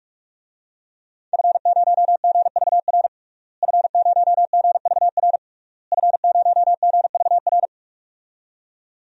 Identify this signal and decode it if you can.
Morse code, heard as a keyed tone, about 42 wpm, 700 Hz. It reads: F0GVR F0GVR F0GVR
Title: DXCC-100 Call Signs - 5 Characters - Repeated 3x - 45wpm